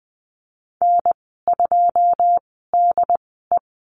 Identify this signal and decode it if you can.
Morse code, heard as a keyed tone, about 19 wpm, spaced as for 14 wpm, 700 Hz. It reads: N2DE
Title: 082 Lesson 14g - Number 2 - Call Signs 20wpm